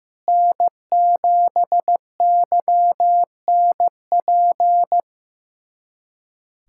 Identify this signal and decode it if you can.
Morse code, heard as a keyed tone, about 15 wpm, 700 Hz. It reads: N7YNP
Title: USA Callsign DE Callsign - 5 Characters - 15wpm